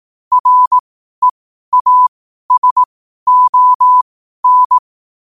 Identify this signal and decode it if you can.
Morse code, heard as a keyed tone, about 18 wpm, spaced as for 12 wpm, 1000 Hz. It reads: REASON